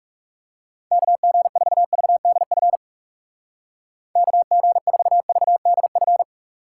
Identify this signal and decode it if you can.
Morse code, heard as a keyed tone, about 38 wpm, 700 Hz. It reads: KG4VDF KG4VDF